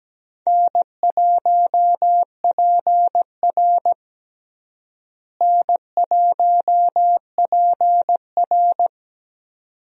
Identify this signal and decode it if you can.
Morse code, heard as a keyed tone, about 17 wpm, 700 Hz. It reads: N1PR N1PR
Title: USA Call Signs - 4 Characters - Repeated 2x - 17wpm